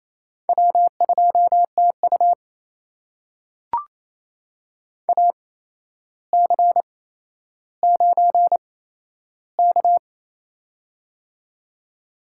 Morse code, heard as a keyed tone, about 28 wpm, spaced as for 12 wpm, 700 Hz. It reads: W2TU AC9K
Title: USA Call Signs - 4 Characters - 8x Character Spacing - 28wpm